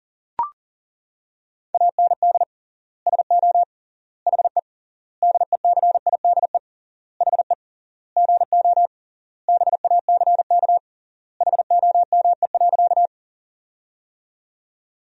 Morse code, heard as a keyed tone, about 40 wpm, 700 Hz. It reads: AND SO HE DECIDE HE GO BACK HOME.
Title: 3x Word-Spacing - Sentences from Top 500 Words - 40wpm